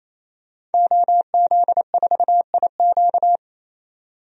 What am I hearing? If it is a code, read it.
Morse code, heard as a keyed tone, about 27 wpm, 700 Hz. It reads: OZ4IQ
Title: DXCC-100 Call Signs - 5 Characters - 28wpm